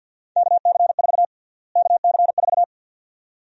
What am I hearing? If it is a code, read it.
Morse code, heard as a keyed tone, about 52 wpm, 700 Hz. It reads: KC4 KC4
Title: USA Call Sign Prefixes - Repeated 2x - 50wpm